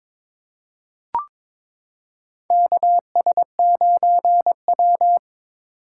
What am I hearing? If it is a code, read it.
Morse code, heard as a keyed tone, about 22 wpm, 700 Hz. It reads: KS9W
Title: K1USN SST QSO Practice - 22wpm